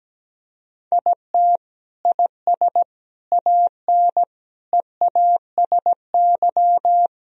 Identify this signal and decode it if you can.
Morse code, heard as a keyed tone, about 17 wpm, 700 Hz. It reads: IT IS AN EASY
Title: Sentences from Top 1000 Words - Review Part 2 - 17wpm